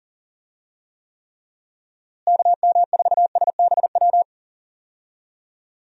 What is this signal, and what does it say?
Morse code, heard as a keyed tone, about 37 wpm, 700 Hz. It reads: KM4SBW
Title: USA Call Signs - 6 Characters - Repeated 3x - 40wpm